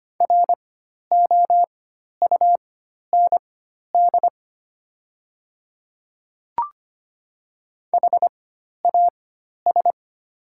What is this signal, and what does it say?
Morse code, heard as a keyed tone, about 25 wpm, spaced as for 12 wpm, 700 Hz. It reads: ROUND HAS
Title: 4x Character Spacing - Top 300 Words - 25wpm